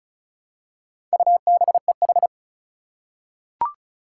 Morse code, heard as a keyed tone, about 36 wpm, 700 Hz. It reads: UBEH
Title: Four Challenging Letter-Number Combinations 35wpm